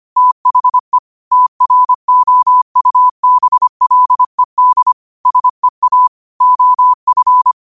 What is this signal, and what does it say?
Morse code, heard as a keyed tone, about 25 wpm, 1000 Hz. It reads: THE TROUBLED SEA OF
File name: 25wpm